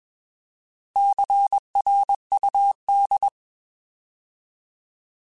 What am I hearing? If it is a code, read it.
Morse code, heard as a keyed tone, about 21 wpm, 800 Hz. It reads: CRUD